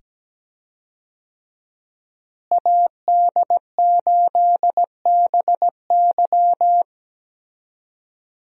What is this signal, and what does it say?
Morse code, heard as a keyed tone, about 17 wpm, 700 Hz. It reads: AD8BY
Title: USA Call Signs - 5 Characters - Repeated 2x - 17wpm